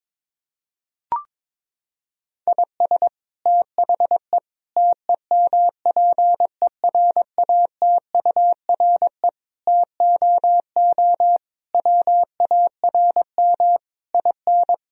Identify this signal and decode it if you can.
Morse code, heard as a keyed tone, about 22 wpm, 700 Hz. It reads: IS THE TEMPERATURE TOO WARM IN
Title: Sentences from Top 1000 Words - Review Part 4 - 22wpm